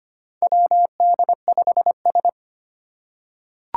Morse code, heard as a keyed tone, about 25 wpm, 700 Hz. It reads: WD5S
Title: Four Challenging Letter-Number Combinations 25wpm